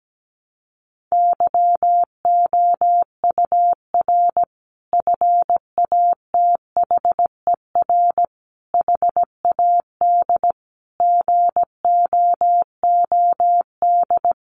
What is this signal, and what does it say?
Morse code, heard as a keyed tone, about 17 wpm, 700 Hz. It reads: YOUR FATHER HAD GOOD
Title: Sentences from Top 1100 Words 17wpm